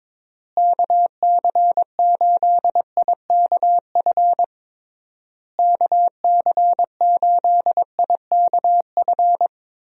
Morse code, heard as a keyed tone, about 21 wpm, 700 Hz. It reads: KC8IKF KC8IKF
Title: USA Call Signs - 6 Characters - Repeated 3x - 22wpm